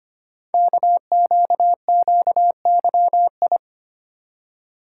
Morse code, heard as a keyed tone, about 25 wpm, 700 Hz. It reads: KQQYI
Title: Code Groups - 5 Letter:Numbers - 25wpm